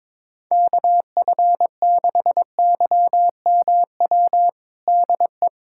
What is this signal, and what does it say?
Morse code, heard as a keyed tone, about 22 wpm, 700 Hz. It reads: KF6YMW DE
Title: USA Callsign DE Callsign 22wpm